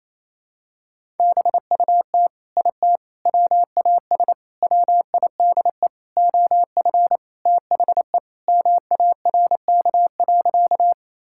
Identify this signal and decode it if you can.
Morse code, heard as a keyed tone, about 28 wpm, 700 Hz. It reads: BUT IT WAS WIDE OF THE MARK.
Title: Sentences from Top 1000 Words - Review Part 6 - 28wpm